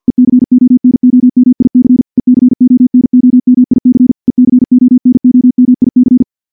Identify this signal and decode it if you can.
Morse code, heard as a keyed tone, about 49 wpm, 250 Hz. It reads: ECONOMIC ECONOMIC ECONOMIC
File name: morse / morse-training-50WPM